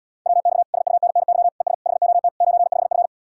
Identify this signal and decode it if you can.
Morse code, heard as a keyed tone, about 80 wpm, 700 Hz. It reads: WX SUNNY ES HOT 95F